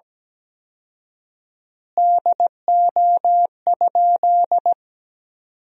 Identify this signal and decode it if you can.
Morse code, heard as a keyed tone, about 17 wpm, 700 Hz. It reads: DO?
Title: 4x Word-Spacing - Sentences from Top 100 Words - 17wpm